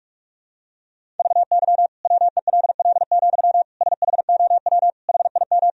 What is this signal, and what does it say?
Morse code, heard as a keyed tone, about 43 wpm, 700 Hz. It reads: VY WELL, SHOW HIM
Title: Sentences from Top 200 Words and Abbreviations 45wpm